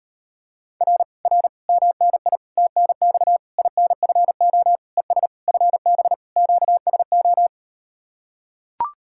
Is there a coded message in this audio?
Yes: Morse code from 0.8 to 7.5 s, heard as a keyed tone, about 38 wpm, 700 Hz. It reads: R R MNI TNX INFO ES FB QSO